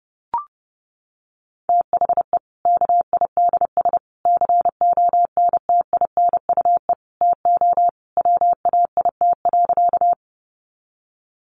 Morse code, heard as a keyed tone, about 30 wpm, 700 Hz. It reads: THE KIDS CONTINUE TO WAIT.
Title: Easy Phrases 30wpm